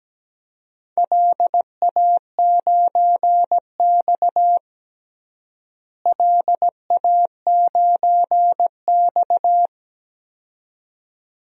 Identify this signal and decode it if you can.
Morse code, heard as a keyed tone, about 17 wpm, 700 Hz. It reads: LA9X LA9X
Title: DXCC-100 Call Signs - 4 Characters - Repeated 2x - 17wpm